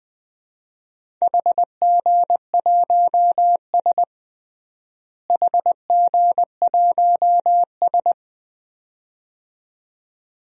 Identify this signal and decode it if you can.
Morse code, heard as a keyed tone, about 20 wpm, 700 Hz. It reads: HG1S HG1S
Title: DXCC-100 Call Signs - 4 Characters - Repeated 2x - 20wpm